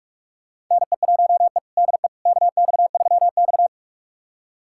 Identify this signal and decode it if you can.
Morse code, heard as a keyed tone, about 41 wpm, 700 Hz. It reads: NE1E DE KX3X
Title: USA Callsign DE Callsign - 4 Characters - 45wpm